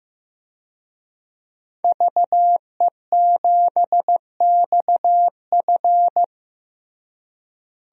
Morse code, heard as a keyed tone, about 15 wpm, 700 Hz. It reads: VE7XF
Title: DXCC-100 Call Signs - 5 Characters - Repeated 2x - 15wpm